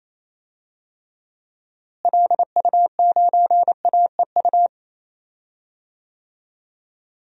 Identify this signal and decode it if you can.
Morse code, heard as a keyed tone, about 28 wpm, 700 Hz. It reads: LU9AEU
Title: DXCC-100 Call Signs - 6 Characters - Repeated 2x - 28wpm